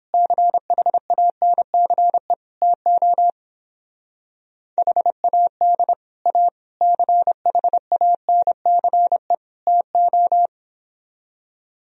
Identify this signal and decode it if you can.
Morse code, heard as a keyed tone, about 29 wpm, 700 Hz. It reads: CHANCE TO HAD A CHANCE TO